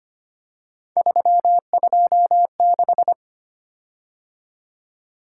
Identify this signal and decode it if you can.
Morse code, heard as a keyed tone, about 25 wpm, 700 Hz. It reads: 326
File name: Three Letter-Number Combinations - 25wpm